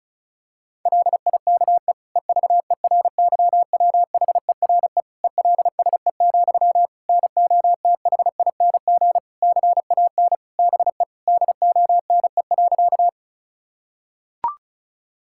Morse code, heard as a keyed tone, about 34 wpm, 700 Hz. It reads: LIKE EVERYWHERE ELSE, NOTHING CAN BE DONE.
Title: Sentences from Top 1600 Words 35wpm